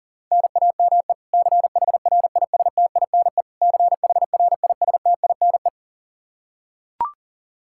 Morse code, heard as a keyed tone, about 40 wpm, 700 Hz. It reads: NAME CHRISTINE CHRISTINE
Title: Practice QSO - Exchange 1 - Beginning - 40wpm